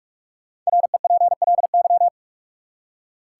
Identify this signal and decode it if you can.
Morse code, heard as a keyed tone, about 46 wpm, 700 Hz. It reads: REPLY
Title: Five Letter Words - 45wpm